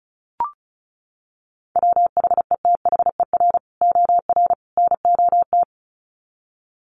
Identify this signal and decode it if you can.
Morse code, heard as a keyed tone, about 34 wpm, 700 Hz. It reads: WHETHER OR NOT
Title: Sets of 3 Words 35wpm